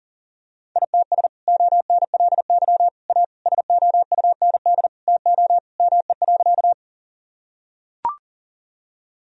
Morse code, heard as a keyed tone, about 37 wpm, 700 Hz. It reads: ITS ONLY A SOUND TO ME.
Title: Sentences from Top 1000 Words - Review Part 1 - 40wpm